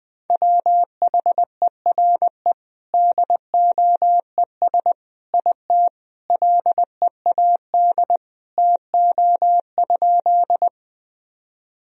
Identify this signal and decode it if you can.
Morse code, heard as a keyed tone, about 20 wpm, 700 Hz. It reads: WHERE DOES IT LEAD TO?
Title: Sentences from Top 1000 Words - Review Part 2 - 20wpm